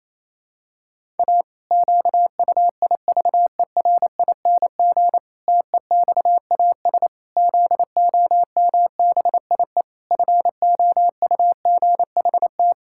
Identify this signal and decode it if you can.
Morse code, heard as a keyed tone, about 28 wpm, 700 Hz. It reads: A QUIVERING TEXAS ZOMBIE FOUGHT
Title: Pangram Challenge v2 - 28wpm